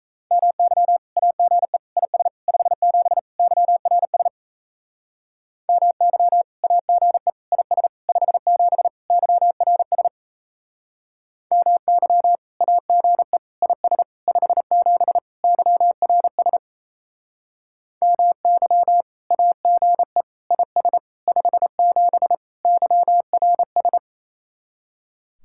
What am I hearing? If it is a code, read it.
Morse code, heard as a keyed tone, about 37 wpm, 700 Hz. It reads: MY AGE IS 57 YRS MY AGE IS 57 YRS MY AGE IS 57 YRS MY AGE IS 57 YRS